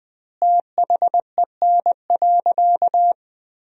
Morse code, heard as a keyed tone, about 21 wpm, 700 Hz. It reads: THEN.
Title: 2x Word-Spacing - Sentences from Top 200 Words - 20wpm